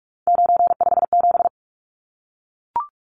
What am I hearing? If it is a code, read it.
Morse code, heard as a keyed tone, about 44 wpm, 700 Hz. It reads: 957
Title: Numbers - 3-Digit 45wpm